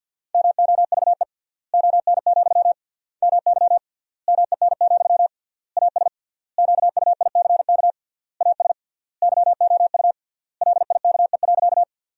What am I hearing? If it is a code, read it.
Morse code, heard as a keyed tone, about 49 wpm, 700 Hz. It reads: MOVE ON, MY MEN, AS QUICK AS YOU LIKE.
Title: 3x Word-Spacing - Sentences from Top 500 Words - 50wpm